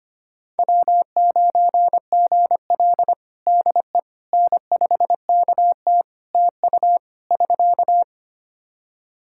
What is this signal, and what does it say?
Morse code, heard as a keyed tone, about 25 wpm, 700 Hz. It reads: W9GL DE N5KT TU <SK>
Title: Practice QSO - Exchange 4 - Ending - 25wpm